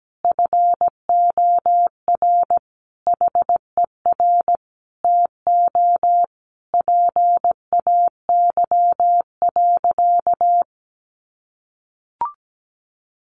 Morse code, heard as a keyed tone, about 17 wpm, 700 Hz. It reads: FOR HER TO PAY.
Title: Sentences from Top 1500 Words 17wpm